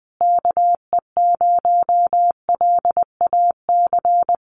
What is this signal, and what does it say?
Morse code, heard as a keyed tone, about 20 wpm, 700 Hz. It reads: KE0LAC
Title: USA Call Signs Repeated 3x 20wpm